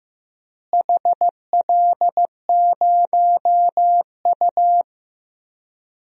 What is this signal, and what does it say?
Morse code, heard as a keyed tone, about 15 wpm, 700 Hz. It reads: HL0U
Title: DXCC-100 Call Signs - 4 Characters - Repeated 2x - 15wpm